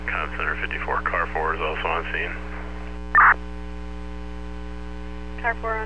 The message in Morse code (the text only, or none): none